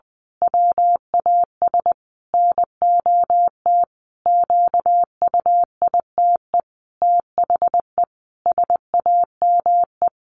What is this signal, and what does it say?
Morse code, heard as a keyed tone, about 20 wpm, 700 Hz. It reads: WAS NOT QUITE THE SAME
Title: A Sound Of Thunder 20wpm